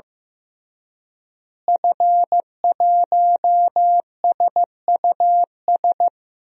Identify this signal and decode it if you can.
Morse code, heard as a keyed tone, about 15 wpm, 700 Hz. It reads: F1SUS
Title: DXCC-100 Call Signs - 5 Characters - Repeated 3x - 15wpm